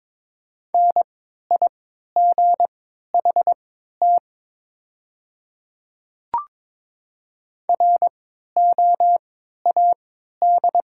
Morse code, heard as a keyed tone, about 23 wpm, spaced as for 13 wpm, 690 Hz. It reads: NIGHT ROAD
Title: 3x Character Spacing - Top 500 Words - 22wpm